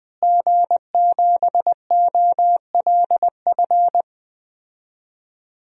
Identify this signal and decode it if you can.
Morse code, heard as a keyed tone, about 20 wpm, 700 Hz. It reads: G7OLF
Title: DXCC-100 Call Signs - 5 Characters - Repeated 2x - 20wpm